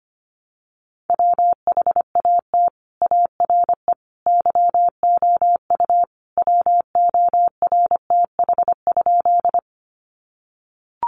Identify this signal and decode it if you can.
Morse code, heard as a keyed tone, about 25 wpm, 700 Hz. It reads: WHAT ARE YOU WORTH?
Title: Sentences from Top 800 Words 25wpm